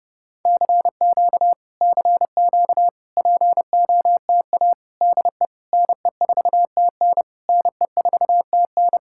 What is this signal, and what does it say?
Morse code, heard as a keyed tone, about 30 wpm, 700 Hz. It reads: CQ CQ POTA DE NE4TN NE4TN